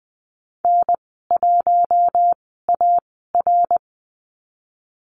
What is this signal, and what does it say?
Morse code, heard as a keyed tone, about 20 wpm, spaced as for 15 wpm, 700 Hz. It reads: N1AR
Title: 147 Lesson 24g - Letter y - Call Signs 20wpm